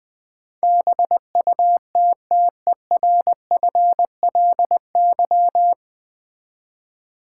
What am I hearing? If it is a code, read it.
Morse code, heard as a keyed tone, about 20 wpm, 700 Hz. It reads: BUTTERFLY
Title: Easy Words with Hints - 20wpm